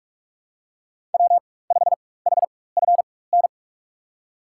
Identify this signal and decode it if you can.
Morse code, heard as a keyed tone, about 43 wpm, spaced as for 21 wpm, 700 Hz. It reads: W5HFN
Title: USA Call Signs - 4x Character Spacing - 45wpm